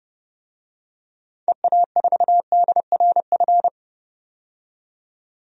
Morse code, heard as a keyed tone, about 30 wpm, 700 Hz. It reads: EA4DRF
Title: DXCC-100 Call Signs - 6 Characters - 30wpm